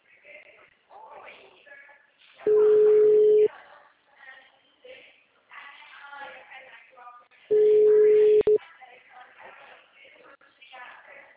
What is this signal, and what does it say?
Ringback tone (Europe, 425 Hz), with speech.
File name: out- -2021- - - .6815